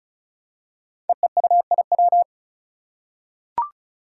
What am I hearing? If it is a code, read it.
Morse code, heard as a keyed tone, about 36 wpm, 700 Hz. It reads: EEUIW
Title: Five Challenging Letter-Number Combinations 35wpm